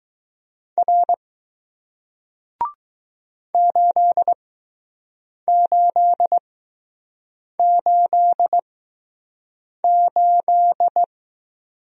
Morse code, heard as a keyed tone, about 21 wpm, spaced as for 6 wpm, 700 Hz. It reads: R 8888